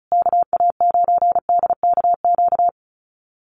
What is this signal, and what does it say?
Morse code, heard as a keyed tone, about 34 wpm, 700 Hz. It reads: KA9DKQ
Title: USA Call Signs Repeated 3x 35wpm